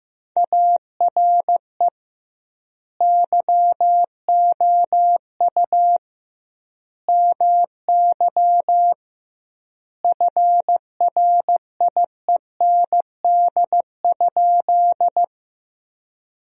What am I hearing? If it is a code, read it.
Morse code, heard as a keyed tone, about 16 wpm, 700 Hz. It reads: ARE YOU MY FRIEND?
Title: 2x Word-Spacing - Sentences from Top 300 Words - 15wpm